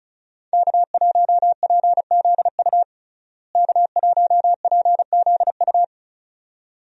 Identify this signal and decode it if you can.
Morse code, heard as a keyed tone, about 36 wpm, 700 Hz. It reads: K1PZU K1PZU
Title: USA Call Signs - 5 Characters - Repeated 2x - 35wpm